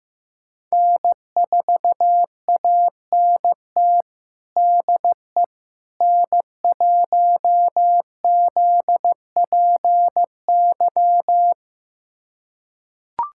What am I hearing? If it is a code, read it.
Morse code, heard as a keyed tone, about 15 wpm, 700 Hz. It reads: N4ANT DE N1ZPY
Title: USA Callsign DE Callsign - 5 Characters - 15wpm